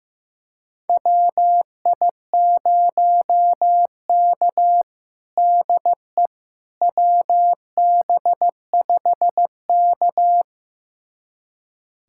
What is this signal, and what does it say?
Morse code, heard as a keyed tone, about 15 wpm, 700 Hz. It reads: WI0K DE WB5K
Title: USA Callsign DE Callsign - 4 Characters - 15wpm